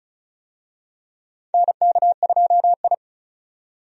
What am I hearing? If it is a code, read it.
Morse code, heard as a keyed tone, about 36 wpm, 700 Hz. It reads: NK2I
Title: USA Call Signs - 4 Characters - 35wpm